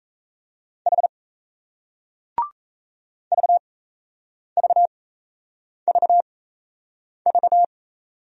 Morse code, heard as a keyed tone, about 38 wpm, spaced as for 8 wpm, 700 Hz. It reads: H VVVV